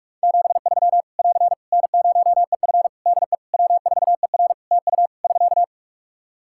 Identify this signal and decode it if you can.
Morse code, heard as a keyed tone, about 44 wpm, 700 Hz. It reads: 73 <AR> N0EF DE W4ER TU <SK>
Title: Practice QSO - Exchange 4 - Ending - 45wpm